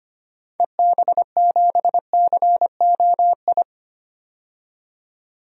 Morse code, heard as a keyed tone, about 25 wpm, 700 Hz. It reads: EB7COI
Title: DXCC-100 Call Signs - 6 Characters - 25wpm